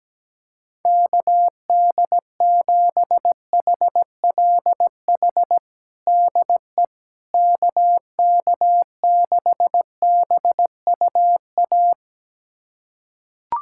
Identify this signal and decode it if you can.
Morse code, heard as a keyed tone, about 17 wpm, 700 Hz. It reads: KD7HLH DE KK6BUA
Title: USA Callsign DE Callsign - 6 Characters - 17wpm